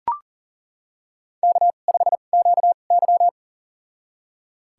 Morse code, heard as a keyed tone, about 40 wpm, spaced as for 29 wpm, 700 Hz. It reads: K5QY
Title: USA Call Signs - 4 Characters - 2x Character Spacing - 40wpm